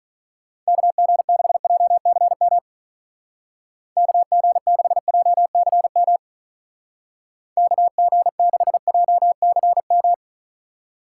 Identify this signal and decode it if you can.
Morse code, heard as a keyed tone, about 38 wpm, 700 Hz. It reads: KG6JCM KG6JCM KG6JCM
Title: Speed Racing - USA 2x3 Call Signs 35wpm